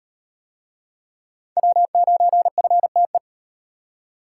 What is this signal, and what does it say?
Morse code, heard as a keyed tone, about 38 wpm, 700 Hz. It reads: W9FTE